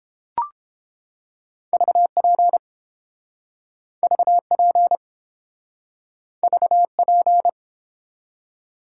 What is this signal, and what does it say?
Morse code, heard as a keyed tone, about 32 wpm, 700 Hz. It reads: VP VP VP